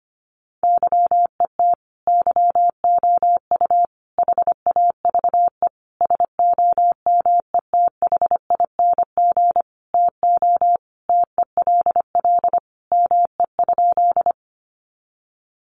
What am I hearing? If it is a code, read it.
Morse code, heard as a keyed tone, about 25 wpm, 700 Hz. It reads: YET YOU HAVE SOMETHING TO TELL ME?
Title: Sentences from Top 600 Words 25wpm